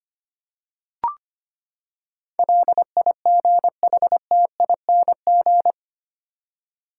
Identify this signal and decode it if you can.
Morse code, heard as a keyed tone, about 25 wpm, 700 Hz. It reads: LIGHTING
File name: Eight Letter Words - 25wpm